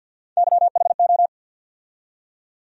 Morse code, heard as a keyed tone, about 51 wpm, 700 Hz. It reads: YHO
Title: USA Call Sign Postfixes - 3 Letters 50wpm